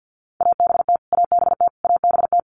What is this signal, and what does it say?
Morse code, heard as a keyed tone, about 49 wpm, 700 Hz. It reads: ABT ABT ABT
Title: Abbreviations Repeated 3x - 50wpm